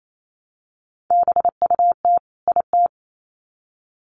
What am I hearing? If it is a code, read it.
Morse code, heard as a keyed tone, about 28 wpm, 700 Hz. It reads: BUT IT
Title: Sets of 2 Words 28wpm